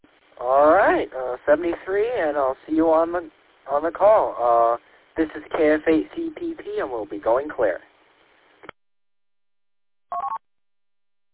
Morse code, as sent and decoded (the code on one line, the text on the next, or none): none